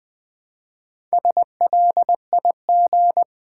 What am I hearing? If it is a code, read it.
Morse code, heard as a keyed tone, about 20 wpm, 700 Hz. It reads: SLIG
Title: Build A Word - 2 Letters At A Time - 6-Letter Words - 20wpm